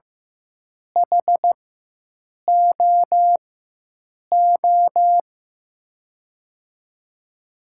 Morse code, heard as a keyed tone, about 15 wpm, spaced as for 7 wpm, 700 Hz. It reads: HOO
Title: USA Call Signs - 5 Characters - 4x Character Spacing - 15wpm